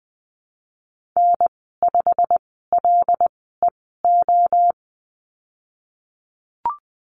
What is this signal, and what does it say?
Morse code, heard as a keyed tone, about 20 wpm, spaced as for 15 wpm, 700 Hz. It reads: N5LEO
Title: 088 Lesson 15g - Number 5 - Call Signs 20wpm